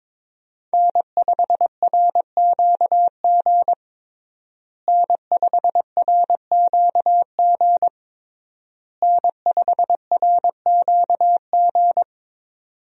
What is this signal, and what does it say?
Morse code, heard as a keyed tone, about 23 wpm, 700 Hz. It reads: N5RQG N5RQG N5RQG
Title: USA Call Signs - 5 Characters - Repeated 3x - 22wpm